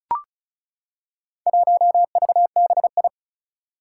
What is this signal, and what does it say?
Morse code, heard as a keyed tone, about 36 wpm, 690 Hz. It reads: 1VBI
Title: Four Challenging Letter-Number Combinations 35wpm